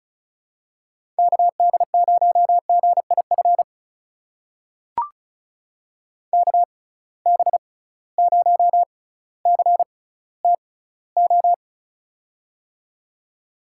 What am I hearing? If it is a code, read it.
Morse code, heard as a keyed tone, about 34 wpm, spaced as for 18 wpm, 700 Hz. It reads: KD0GIF KB0CTO
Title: USA Call Signs - 6 Characters - 6x Character Spacing - 35wpm